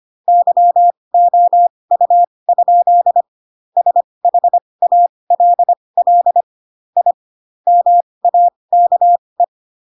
Morse code, heard as a keyed tone, about 25 wpm, spaced as for 20 wpm, 700 Hz. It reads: YOU? SHALL I MAKE